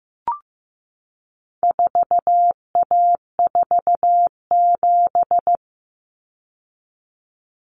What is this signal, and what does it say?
Morse code, heard as a keyed tone, about 15 wpm, 700 Hz. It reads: 4A47
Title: Common Cut Numbers - 4-Digit 15wpm